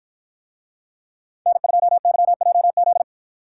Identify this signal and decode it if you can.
Morse code, heard as a keyed tone, about 49 wpm, 700 Hz. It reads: N2YJZ